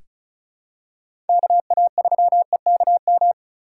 Morse code, heard as a keyed tone, about 36 wpm, 700 Hz. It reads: KA3EKM
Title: USA Call Signs - 4x Character Spacing - 35wpm